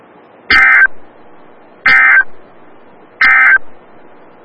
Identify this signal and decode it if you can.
Emergency Alert System (SAME) tones, heard as harsh buzzing bursts. It is heard from 0.4 to 3.7 s.
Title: Des Moines, IA NOAA Weather Radio - WXL57